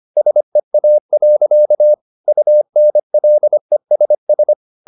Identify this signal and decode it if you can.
Morse code, heard as a keyed tone, about 25 wpm, 600 Hz. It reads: SEA. UNLESS
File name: the old man and the sea part 1 25wpm